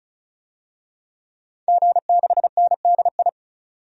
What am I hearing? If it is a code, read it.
Morse code, heard as a keyed tone, about 36 wpm, 700 Hz. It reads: G6NDI
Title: DXCC-100 Call Signs - 5 Characters - 35wpm